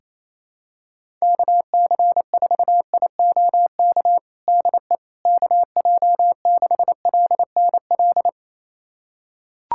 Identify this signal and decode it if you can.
Morse code, heard as a keyed tone, about 28 wpm, 700 Hz. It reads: KC4IOK DE KJ6LNL
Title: USA Callsign DE Callsign - 6 Characters - 28wpm